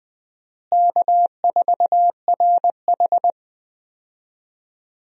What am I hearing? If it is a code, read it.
Morse code, heard as a keyed tone, about 20 wpm, 700 Hz. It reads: K4RH
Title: USA Call Signs - 4 Characters - Repeated 2x - 20wpm